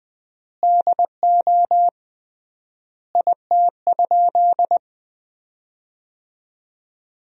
Morse code, heard as a keyed tone, about 20 wpm, 700 Hz. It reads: DO IT?
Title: 3x Word-Spacing - Sentences from Top 100 Words - 20wpm